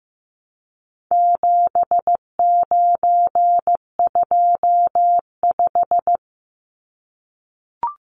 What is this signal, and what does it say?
Morse code, heard as a keyed tone, about 15 wpm, 700 Hz. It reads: 7925
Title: Numbers - 4-Digit 15wpm